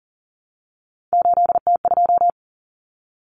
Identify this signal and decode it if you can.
Morse code, heard as a keyed tone, about 40 wpm, 700 Hz. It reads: 8T2
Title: Common Cut Numbers - 3-Digit 40wpm